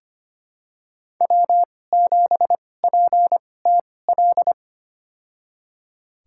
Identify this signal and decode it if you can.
Morse code, heard as a keyed tone, about 25 wpm, spaced as for 18 wpm, 700 Hz. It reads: W7PTL
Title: USA Call Signs - 5 Characters - 2x Character Spacing - 25wpm